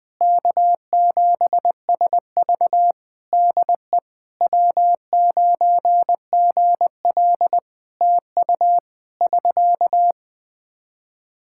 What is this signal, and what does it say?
Morse code, heard as a keyed tone, about 20 wpm, 700 Hz. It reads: K7SV DE W9GL TU <SK>
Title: Practice QSO - Exchange 4 - Ending - 20wpm